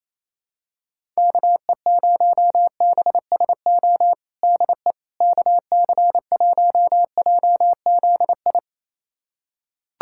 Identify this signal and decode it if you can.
Morse code, heard as a keyed tone, about 28 wpm, 700 Hz. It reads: KE0BSO DE KC1JZI
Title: USA Callsign DE Callsign - 6 Characters - 28wpm